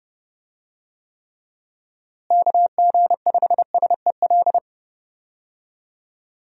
Morse code, heard as a keyed tone, about 30 wpm, 700 Hz. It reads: KG5SEL
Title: USA Call Signs - 6 Characters - Repeated 3x - 30wpm